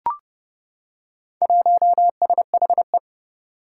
Morse code, heard as a keyed tone, about 31 wpm, 700 Hz. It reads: 1SHE
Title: Four Challenging Letter-Number Combinations 30wpm